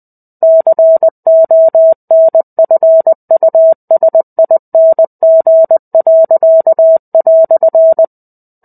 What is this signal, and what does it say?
Morse code, heard as a keyed tone, about 20 wpm, 650 Hz. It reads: CONFUSING."
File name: 05-Chapter-05 20 WPM